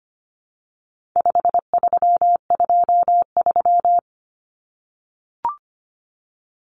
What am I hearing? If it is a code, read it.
Morse code, heard as a keyed tone, about 25 wpm, 700 Hz. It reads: 5323
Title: Numbers - 4-Digit 25wpm